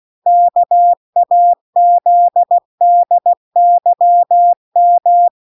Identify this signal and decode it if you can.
Morse code, heard as a keyed tone, about 16 wpm, 700 Hz. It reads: KAZDYM